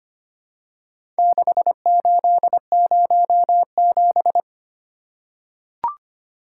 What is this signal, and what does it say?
Morse code, heard as a keyed tone, about 25 wpm, 700 Hz. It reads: 6807
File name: practice / 25wpm / Four Letter-Number Combinations - 25wpm